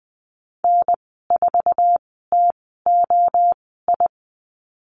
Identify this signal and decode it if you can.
Morse code, heard as a keyed tone, about 20 wpm, spaced as for 14 wpm, 700 Hz. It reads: N4TOI
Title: 088 Lesson 15g - Number 5 - Call Signs 20wpm